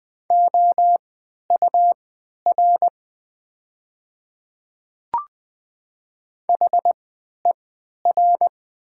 Morse code, heard as a keyed tone, about 20 wpm, spaced as for 11 wpm, 700 Hz. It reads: OUR HER
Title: 3x Character Spacing - Top 500 Words - 20wpm